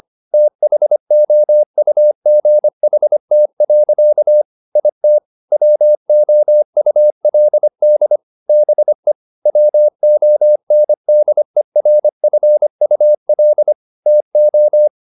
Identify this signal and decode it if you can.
Morse code, heard as a keyed tone, about 25 wpm, 600 Hz. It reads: THOUGHT. IT WOULD BE WONDERFUL TO
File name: the old man and the sea part 1 25wpm